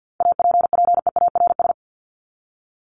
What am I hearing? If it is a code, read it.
Morse code, heard as a keyed tone, about 50 wpm, 700 Hz. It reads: APPEARS
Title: ES:S Suffix - Top 100 Words with Base Words 50wpm